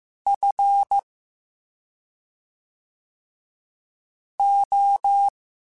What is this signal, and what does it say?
Morse code, heard as a keyed tone, about 15 wpm, 800 Hz. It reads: F O